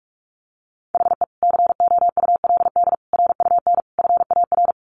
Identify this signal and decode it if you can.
Morse code, heard as a keyed tone, about 44 wpm, 700 Hz. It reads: HE COULD RUN FAR
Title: Sentences from Top 1100 Words 45wpm